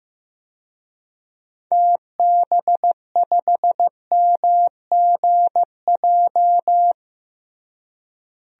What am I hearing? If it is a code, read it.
Morse code, heard as a keyed tone, about 15 wpm, 700 Hz. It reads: TB5MGJ
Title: DXCC-100 Call Signs - 6 Characters - Repeated 3x - 15wpm